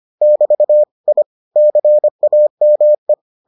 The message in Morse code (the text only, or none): <BT> I CAME